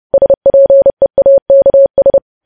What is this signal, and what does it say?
Morse code, heard as a keyed tone, about 30 wpm, 550 Hz. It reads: SPEAKS